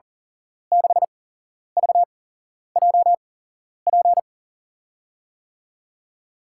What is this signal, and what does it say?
Morse code, heard as a keyed tone, about 40 wpm, spaced as for 11 wpm, 700 Hz. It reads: 6VJP
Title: USA Call Signs - 6 Characters - 8x Character Spacing - 40wpm